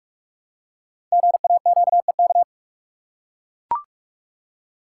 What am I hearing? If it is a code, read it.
Morse code, heard as a keyed tone, about 44 wpm, 700 Hz. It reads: GAQEK
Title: Code Groups - 5 Letters - 45wpm